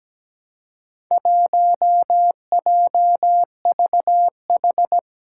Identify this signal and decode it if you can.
Morse code, heard as a keyed tone, about 17 wpm, 700 Hz. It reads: 1JVH
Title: Four Challenging Letter-Number Combinations 17wpm